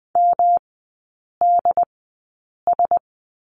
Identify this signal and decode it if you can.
Morse code, heard as a keyed tone, about 20 wpm, spaced as for 8 wpm, 700 Hz. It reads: MDS
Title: 133 Lesson 22e - Question Mark - Three Characters 20wpm